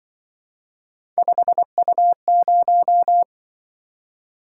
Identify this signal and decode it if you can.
Morse code, heard as a keyed tone, about 23 wpm, 700 Hz. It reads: 5U0
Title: Speed-Racing - Three Letter-Number Combinations - 20wpm